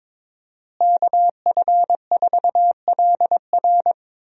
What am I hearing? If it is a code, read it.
Morse code, heard as a keyed tone, about 22 wpm, 700 Hz. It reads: KF4LR
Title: USA Call Signs - 5 Characters - Repeated 3x - 22wpm